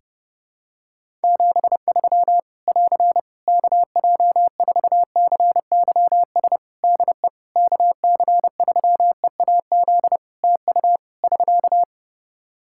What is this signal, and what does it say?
Morse code, heard as a keyed tone, about 31 wpm, 700 Hz. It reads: 73 <AR> KJ4CYS DE KC3EAZ TU <SK>
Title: Practice QSO - Exchange 4 - Ending - 30wpm